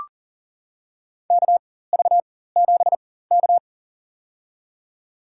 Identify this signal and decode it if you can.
Morse code, heard as a keyed tone, about 40 wpm, spaced as for 19 wpm, 700 Hz. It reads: KV7K
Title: USA Call Signs - 4 Characters - 4x Character Spacing - 40wpm